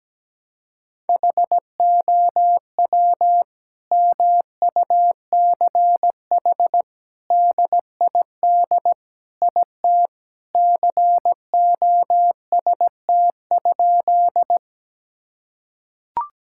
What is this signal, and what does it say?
Morse code, heard as a keyed tone, about 17 wpm, 700 Hz. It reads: HOW MUCH DID IT COST?
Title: Sentences from Top 1000 Words - Review Part 5 - 17wpm